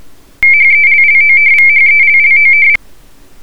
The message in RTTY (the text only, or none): HELLOVHELLOV